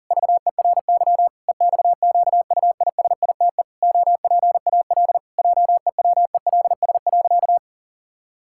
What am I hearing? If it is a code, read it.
Morse code, heard as a keyed tone, about 40 wpm, 700 Hz. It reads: VERY EXQUISITE OPAL JEWELS.
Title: Pangram Challenge v2 - 40wpm